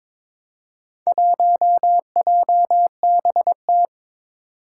Morse code, heard as a keyed tone, about 22 wpm, 700 Hz. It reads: 1JBT
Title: Four Challenging Letter-Number Combinations 22wpm